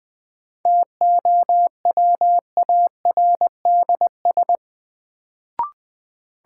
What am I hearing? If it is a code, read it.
Morse code, heard as a keyed tone, about 20 wpm, 700 Hz. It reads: TOWARDS
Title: Sentences from Top 1000 Words - Review Part 2 - New Word Drill - 20wpm